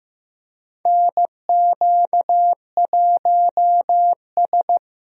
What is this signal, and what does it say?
Morse code, heard as a keyed tone, about 15 wpm, 700 Hz. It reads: NQ1S
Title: USA Call Signs - 4 Characters - Repeated 2x - 15wpm